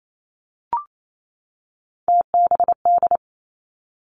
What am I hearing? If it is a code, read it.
Morse code, heard as a keyed tone, about 28 wpm, 700 Hz. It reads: TBD
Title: Three Challenging Letter-Number Combinations 28wpm